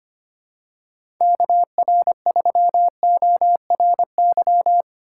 Morse code, heard as a keyed tone, about 25 wpm, 700 Hz. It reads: KR3ORY
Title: USA Call Signs - 6x Character Spacing - 25wpm